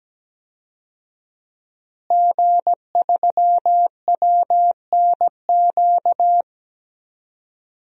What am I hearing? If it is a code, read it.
Morse code, heard as a keyed tone, about 17 wpm, 700 Hz. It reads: G3WNQ
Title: DXCC-100 Call Signs - 5 Characters - Repeated 3x - 17wpm